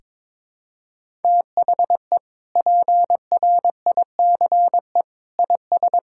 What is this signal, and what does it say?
Morse code, heard as a keyed tone, about 22 wpm, 700 Hz. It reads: THE PRICE IS
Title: Sentences from Top 1000 Words - Review Part 6 - 22wpm